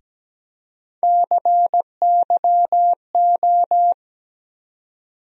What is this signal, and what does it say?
Morse code, heard as a keyed tone, about 17 wpm, 700 Hz. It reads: CYO
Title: USA Call Sign Postfixes - 3 Letters 17wpm